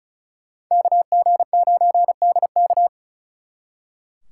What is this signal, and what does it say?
Morse code, heard as a keyed tone, about 44 wpm, 700 Hz. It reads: KG9DK